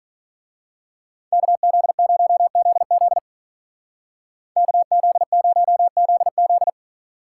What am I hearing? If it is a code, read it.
Morse code, heard as a keyed tone, about 40 wpm, 700 Hz. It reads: KZ0ZZ KZ0ZZ
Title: Speed Racing - USA 2x2 Call Signs 35wpm